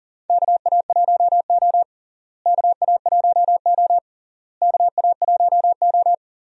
Morse code, heard as a keyed tone, about 40 wpm, 700 Hz. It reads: KA1O KA1O KA1O
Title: USA Call Signs - 4 Characters - Repeated 3x - 40wpm